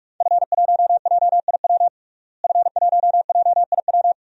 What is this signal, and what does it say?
Morse code, heard as a keyed tone, about 46 wpm, 700 Hz. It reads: F1JIW F1JIW
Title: DXCC-100 Call Signs - 5 Characters - Repeated 3x - 45wpm